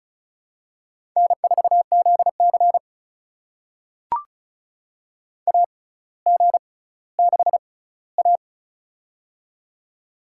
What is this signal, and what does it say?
Morse code, heard as a keyed tone, about 36 wpm, spaced as for 18 wpm, 700 Hz. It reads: N4ZC AG6A
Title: USA Call Signs - 4 Characters - 6x Character Spacing - 35wpm